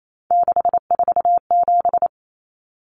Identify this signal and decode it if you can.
Morse code, heard as a keyed tone, about 28 wpm, 700 Hz. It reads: 647
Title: Numbers - 3-Digit 28wpm